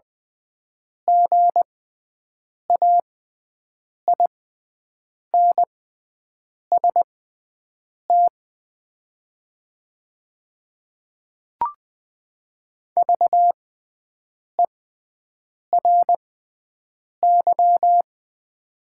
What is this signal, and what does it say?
Morse code, heard as a keyed tone, about 20 wpm, spaced as for 7 wpm, 700 Hz. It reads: GAINST VERY
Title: 6x Character Spacing - Top 500 Words - 20wpm